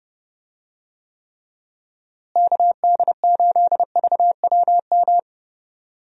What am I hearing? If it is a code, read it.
Morse code, heard as a keyed tone, about 32 wpm, 700 Hz. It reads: KD8VWM
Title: USA Call Signs - 6 Characters - Repeated 2x - 30wpm